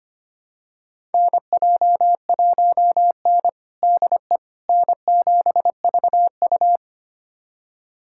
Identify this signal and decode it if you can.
Morse code, heard as a keyed tone, about 25 wpm, 700 Hz. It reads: NJ1N DE N7VU
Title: USA Callsign DE Callsign - 4 Characters - 25wpm